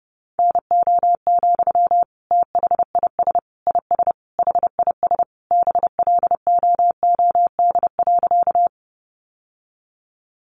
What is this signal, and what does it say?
Morse code, heard as a keyed tone, about 30 wpm, 700 Hz. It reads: NO, THIS IS HIS BLOOD.
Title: Sentences from Top 1100 Words 30wpm